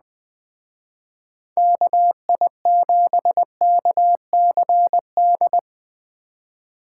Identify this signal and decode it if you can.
Morse code, heard as a keyed tone, about 20 wpm, 700 Hz. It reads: KI7KCD
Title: Speed Racing - USA 2x3 Call Signs 17wpm